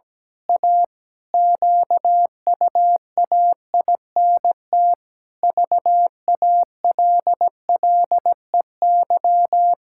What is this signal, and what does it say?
Morse code, heard as a keyed tone, about 17 wpm, 700 Hz. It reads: A QUAINT VALLEY
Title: Pangram Challenge v2 - 17wpm